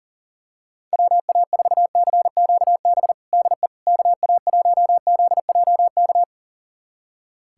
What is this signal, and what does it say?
Morse code, heard as a keyed tone, about 37 wpm, 700 Hz. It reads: WA4CQB DE KA1ZJK
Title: USA Callsign DE Callsign - 6 Characters - 40wpm